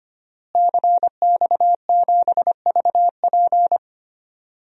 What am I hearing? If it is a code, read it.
Morse code, heard as a keyed tone, about 25 wpm, 700 Hz. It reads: CX7VP
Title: DXCC-100 Call Signs - 5 Characters - Repeated 2x - 25wpm